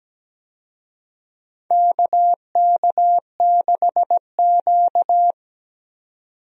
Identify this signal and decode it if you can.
Morse code, heard as a keyed tone, about 17 wpm, 700 Hz. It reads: KK6Q
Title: USA Call Signs - 4 Characters - 17wpm